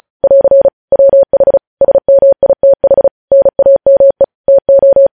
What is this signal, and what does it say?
Morse code, heard as a keyed tone, about 36 wpm, 550 Hz. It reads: <AR> WH SMITH NAME TO
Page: News headlines in Morse code at 35 wpm